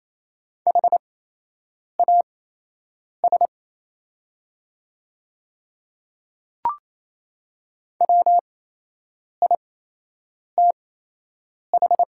Morse code, heard as a keyed tone, about 28 wpm, spaced as for 8 wpm, 700 Hz. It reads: HAS WITH
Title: 8x Character Spacing - Top 100 Words - 28wpm